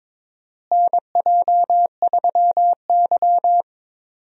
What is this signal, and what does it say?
Morse code, heard as a keyed tone, about 22 wpm, 700 Hz. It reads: NJ3Y
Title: USA Call Signs - 4 Characters - Repeated 3x - 22wpm